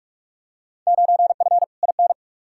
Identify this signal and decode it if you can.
Morse code, heard as a keyed tone, about 41 wpm, 700 Hz. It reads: 9F IN
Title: Field Day Exchanges Encore - 45wpm